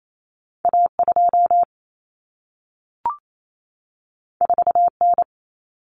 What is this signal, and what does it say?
Morse code, heard as a keyed tone, about 28 wpm, 700 Hz. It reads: A2 4N
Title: Common Cut Numbers - 2-Digit 28wpm